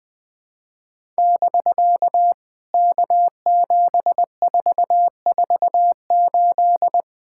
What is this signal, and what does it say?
Morse code, heard as a keyed tone, about 20 wpm, 700 Hz. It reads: <BK> K7448
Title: POTA QSO Practice - 20wpm